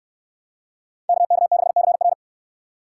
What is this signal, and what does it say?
Morse code, heard as a keyed tone, about 68 wpm, 700 Hz. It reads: KK6CR